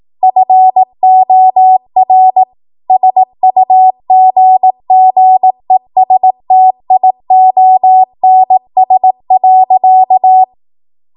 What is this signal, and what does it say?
Morse code, heard as a keyed tone, about 18 wpm, 750 Hz. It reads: FOR SUGGESTIONS.
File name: Morse / 18WPM